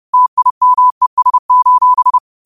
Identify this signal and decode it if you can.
Morse code, heard as a keyed tone, about 30 wpm, 1000 Hz. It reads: TIMES:
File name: qotdfiles / 30wpm